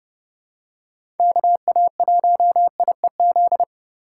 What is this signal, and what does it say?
Morse code, heard as a keyed tone, about 30 wpm, 700 Hz. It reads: KA1IEZ
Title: USA Call Signs - 2x Character Spacing - 30wpm